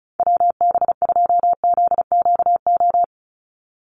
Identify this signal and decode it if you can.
Morse code, heard as a keyed tone, about 34 wpm, 700 Hz. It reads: WB2ZQO
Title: USA Call Signs 35wpm